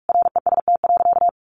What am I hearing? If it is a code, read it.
Morse code, heard as a keyed tone, about 44 wpm, 700 Hz. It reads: REST.
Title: Sentences from Top 600 Words 45wpm